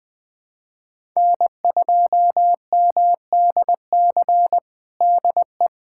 Morse code, heard as a keyed tone, about 20 wpm, 700 Hz. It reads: N2MDC DE
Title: USA Callsign DE Callsign - 5 Characters - 20wpm